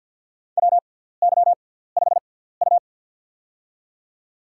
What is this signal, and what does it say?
Morse code, heard as a keyed tone, about 53 wpm, spaced as for 17 wpm, 700 Hz. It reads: WY5U
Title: USA Call Signs - 4 Characters - 6x Character Spacing - 50wpm